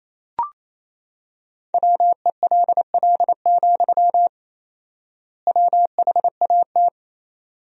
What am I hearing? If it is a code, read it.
Morse code, heard as a keyed tone, about 28 wpm, 700 Hz. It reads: WELL, WHAT
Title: 4x Word-Spacing - Sentences from Top 300 Words - 28wpm